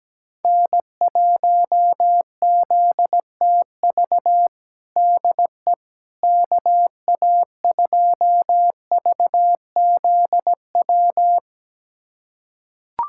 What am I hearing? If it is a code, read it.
Morse code, heard as a keyed tone, about 17 wpm, 700 Hz. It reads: N1ZTV DE KA2VZW
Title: USA Callsign DE Callsign 17wpm